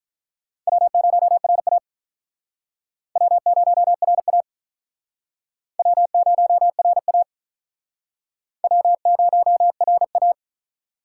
Morse code, heard as a keyed tone, about 48 wpm, 700 Hz. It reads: W0RA W0RA W0RA W0RA